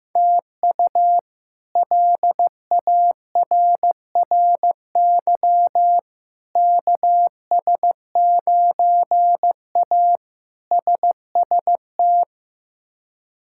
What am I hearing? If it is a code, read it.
Morse code, heard as a keyed tone, about 16 wpm, 700 Hz. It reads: TU LARRY KS9A SST
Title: K1USN SST QSO Practice - 15wpm